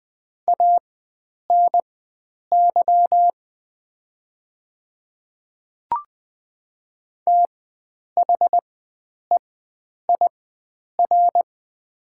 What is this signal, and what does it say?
Morse code, heard as a keyed tone, about 20 wpm, spaced as for 9 wpm, 700 Hz. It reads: ANY THEIR
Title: 4x Character Spacing - Top 300 Words - 20wpm